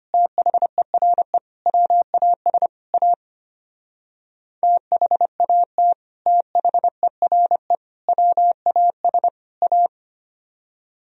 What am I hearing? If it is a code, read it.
Morse code, heard as a keyed tone, about 31 wpm, 700 Hz. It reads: THERE WAS A THAT THERE WAS A
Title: Speed-Racing - Sets of 4 Words 25wpm